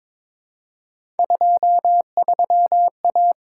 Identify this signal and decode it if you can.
Morse code, heard as a keyed tone, about 22 wpm, 700 Hz. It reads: 23A
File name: practice / 22wpm / Three Letter-Number Combinations - 22wpm